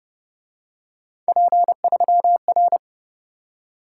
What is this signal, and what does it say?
Morse code, heard as a keyed tone, about 30 wpm, 700 Hz. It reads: P3R
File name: Speed-Racing - Three Letter-Number Combinations - 25wpm